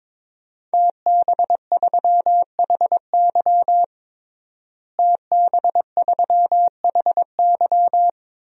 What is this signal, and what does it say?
Morse code, heard as a keyed tone, about 22 wpm, 700 Hz. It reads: TB3HY TB3HY
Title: DXCC-100 Call Signs - 5 Characters - Repeated 3x - 22wpm